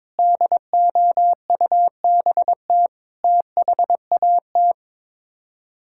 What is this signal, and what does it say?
Morse code, heard as a keyed tone, about 22 wpm, 700 Hz. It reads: DOUBT THAT
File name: practice / 22wpm / Sets of 7 Words - 22wpm